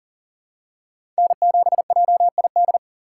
Morse code, heard as a keyed tone, about 40 wpm, 700 Hz. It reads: N7JID
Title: USA Call Signs - 5 Characters - 2x Character Spacing - 40wpm